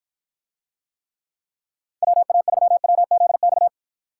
Morse code, heard as a keyed tone, about 52 wpm, 700 Hz. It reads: WA3WZX